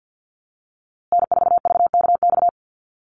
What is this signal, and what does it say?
Morse code, heard as a keyed tone, about 50 wpm, 700 Hz. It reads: N4VKX
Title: USA Call Signs Repeated 3x 50wpm